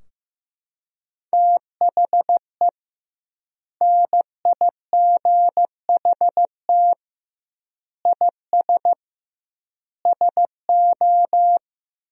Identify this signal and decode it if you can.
Morse code, heard as a keyed tone, about 15 wpm, 700 Hz. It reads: THE NIGHT IS SO
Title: 2x Word-Spacing - Sentences from Top 500 Words - 15wpm